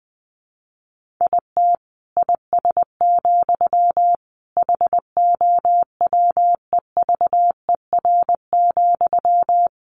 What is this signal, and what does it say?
Morse code, heard as a keyed tone, about 20 wpm, 700 Hz. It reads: IT IS, HOWEVER,
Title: Sentences from Top 1400 Words 20wpm